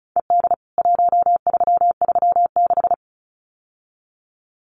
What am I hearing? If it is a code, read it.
Morse code, heard as a keyed tone, about 34 wpm, 700 Hz. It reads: ED 1336
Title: CWT Exchange 35wpm